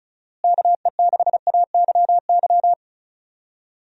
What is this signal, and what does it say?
Morse code, heard as a keyed tone, about 36 wpm, 700 Hz. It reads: KE6AYY
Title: USA Call Signs - 6 Characters - Repeated 2x - 35wpm